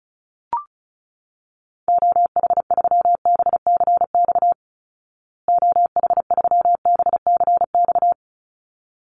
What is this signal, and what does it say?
Morse code, heard as a keyed tone, about 34 wpm, 700 Hz. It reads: OH3BCX OH3BCX
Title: Call Signs Repeated 3x - 35wpm